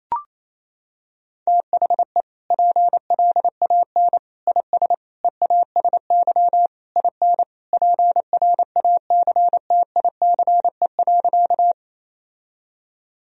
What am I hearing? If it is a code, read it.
Morse code, heard as a keyed tone, about 28 wpm, 700 Hz. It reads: THE PLAN IS EASY IN PRACTICE.
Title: Sentences from Top 1000 Words - Review Part 4 - 28wpm